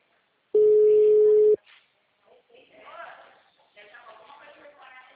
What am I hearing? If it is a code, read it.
Ringback tone (Europe, 425 Hz), with speech.
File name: out- -2002- - - .574